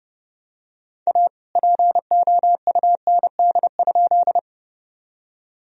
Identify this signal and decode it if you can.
Morse code, heard as a keyed tone, about 31 wpm, 700 Hz. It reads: A POUND?
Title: Sentences from Top 1000 Words - Review Part 2 - 30wpm